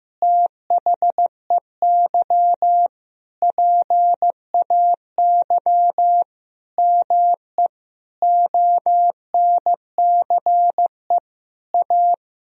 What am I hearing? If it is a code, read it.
Morse code, heard as a keyed tone, about 15 wpm, 700 Hz. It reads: THEY PAY ME ONCE A
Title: Sentences from Top 1000 Words - Review Part 1 - 15wpm